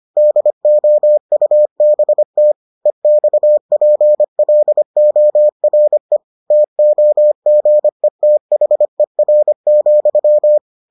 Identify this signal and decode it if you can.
Morse code, heard as a keyed tone, about 25 wpm, 600 Hz. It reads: DOUBT EXPLORE TOGETHER,